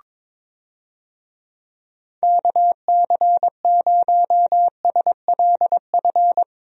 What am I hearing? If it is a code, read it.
Morse code, heard as a keyed tone, about 22 wpm, 700 Hz. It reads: KC0SLF
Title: USA Call Signs - 6 Characters - Repeated 3x - 22wpm